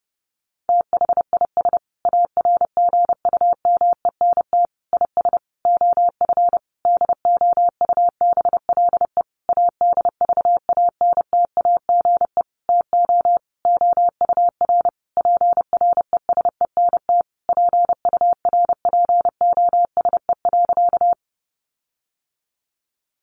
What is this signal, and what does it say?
Morse code, heard as a keyed tone, about 30 wpm, 700 Hz. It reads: THIS ARGUMENT IS OF DOUBLE ADVANTAGE TO OUR PRESENT PURPOSE.
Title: Sentences from Top 1400 Words 30wpm